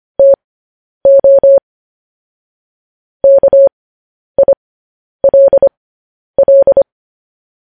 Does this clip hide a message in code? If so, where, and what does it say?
Yes: Morse code from 0.2 to 6.8 s, heard as a keyed tone, about 25 wpm, spaced as for 10 wpm, 550 Hz. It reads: TO KILL